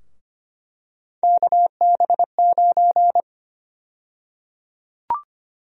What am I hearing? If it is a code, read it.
Morse code, heard as a keyed tone, about 25 wpm, 700 Hz. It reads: KB9
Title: USA Call Sign Prefixes - Repeated 2x - 25wpm